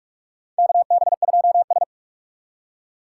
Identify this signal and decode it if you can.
Morse code, heard as a keyed tone, about 45 wpm, 700 Hz. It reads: KB2S
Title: K1USN SST QSO Practice - 45wpm